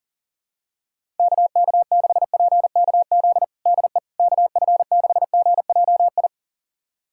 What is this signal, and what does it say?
Morse code, heard as a keyed tone, about 40 wpm, 700 Hz. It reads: KK6PKZ DE KF6GJI
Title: USA Callsign DE Callsign 40wpm